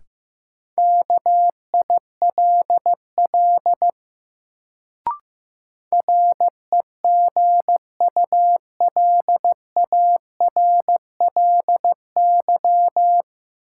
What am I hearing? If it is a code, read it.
Morse code, heard as a keyed tone, about 15 wpm, 700 Hz. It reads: KILL REGULARLY
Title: Sentences from Top 1400 Words - New Word Drill - 15wpm